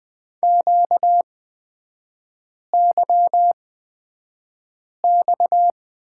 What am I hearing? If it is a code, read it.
Morse code, heard as a keyed tone, about 20 wpm, spaced as for 5 wpm, 700 Hz. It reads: QYX
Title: 236 Lesson 38e - Letter q - Three Characters 20wpm